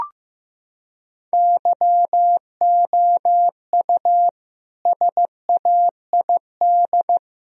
Morse code, heard as a keyed tone, about 14 wpm, 700 Hz. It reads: YOU SAID
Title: Sentences from Top 1000 Words - Review Part 1 - 15wpm